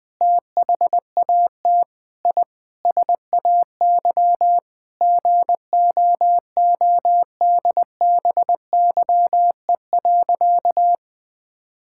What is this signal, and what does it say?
Morse code, heard as a keyed tone, about 20 wpm, 700 Hz. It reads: THAT I SAY GOODBYE.
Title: Sentences with 7-Word Phrases - Spoken Beginning - 20wpm